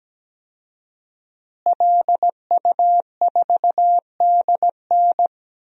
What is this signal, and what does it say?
Morse code, heard as a keyed tone, about 17 wpm, 700 Hz. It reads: LU4DN
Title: DXCC-100 Call Signs - 5 Characters - Repeated 3x - 17wpm